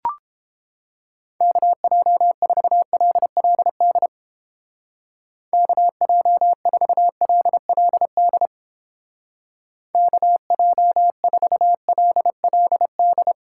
Morse code, heard as a keyed tone, about 31 wpm, 700 Hz. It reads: KJ4LLD KJ4LLD KJ4LLD